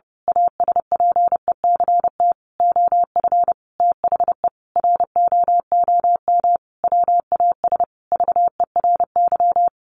Morse code, heard as a keyed tone, about 30 wpm, 700 Hz. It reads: ASPECT OF THE ROOM WAS VERY
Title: Sentences from Top 1300 Words 30wpm